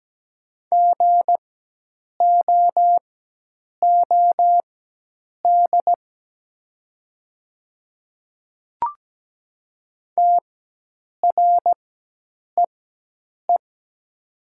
Morse code, heard as a keyed tone, about 17 wpm, spaced as for 8 wpm, 700 Hz. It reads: GOOD TREE
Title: 4x Character Spacing - Top 300 Words - 17wpm